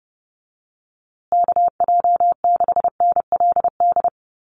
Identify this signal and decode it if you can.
Morse code, heard as a keyed tone, about 30 wpm, 700 Hz. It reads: KJ6NLD
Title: USA Call Signs Repeated 2x 30wpm